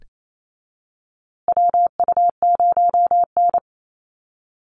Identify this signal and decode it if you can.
Morse code, heard as a keyed tone, about 28 wpm, 700 Hz. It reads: WU0N
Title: USA Call Signs Repeated 2x 28wpm